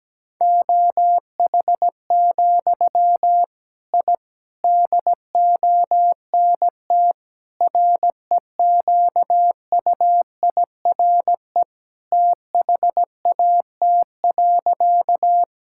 Morse code, heard as a keyed tone, about 17 wpm, 700 Hz. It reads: OH, I DONT REQUIRE THAT.
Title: Sentences from Top 1000 Words - Review Part 6 - 17wpm